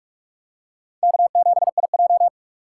Morse code, heard as a keyed tone, about 44 wpm, 700 Hz. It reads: K7IJ